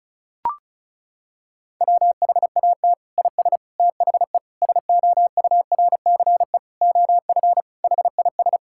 Morse code, heard as a keyed tone, about 36 wpm, 700 Hz. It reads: WHAT IS THE SOURCE OF HIS
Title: Sentences from Top 1000 Words - Review Part 5 - 35wpm